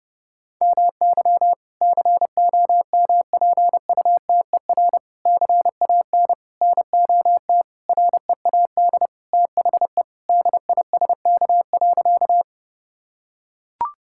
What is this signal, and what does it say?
Morse code, heard as a keyed tone, about 30 wpm, 700 Hz. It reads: MY COMPUTER CAN NOT READ THE DISK.
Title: Sentences from Top 1000 Words - Review Part 6 - 30wpm